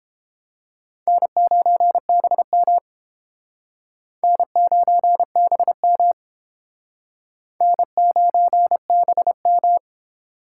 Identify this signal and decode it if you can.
Morse code, heard as a keyed tone, about 32 wpm, 700 Hz. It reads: N9BM N9BM N9BM